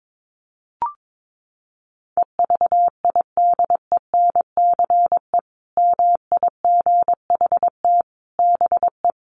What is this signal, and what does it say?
Morse code, heard as a keyed tone, about 22 wpm, 700 Hz. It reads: EVIDENCE MIGHT BE
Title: Sentences from Top 1300 Words 22wpm